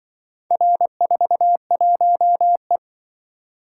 Morse code, heard as a keyed tone, about 24 wpm, 700 Hz. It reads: R41E
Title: Speed-Racing - Four Letter-Number Combinations - 20wpm